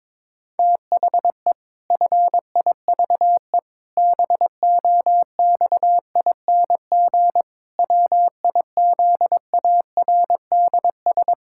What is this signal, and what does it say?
Morse code, heard as a keyed tone, about 22 wpm, 700 Hz. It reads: THE FIVE BOXING WIZARDS
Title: Pangram Challenge v2 - 22wpm